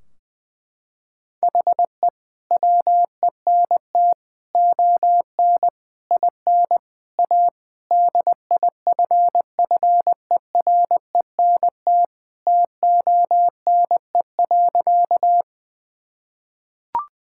Morse code, heard as a keyed tone, about 20 wpm, 700 Hz. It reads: HE WENT ON IN A DIFFERENT TONE.
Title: Sentences from Top 1600 Words 20wpm